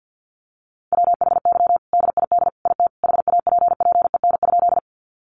Morse code, heard as a keyed tone, about 50 wpm, 700 Hz. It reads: WHY DID IT HAPPEN?
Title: Sentences from Top 400 Words 50wpm